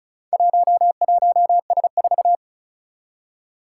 Morse code, heard as a keyed tone, about 36 wpm, 700 Hz. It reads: 11S4
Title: Four Challenging Letter-Number Combinations 35wpm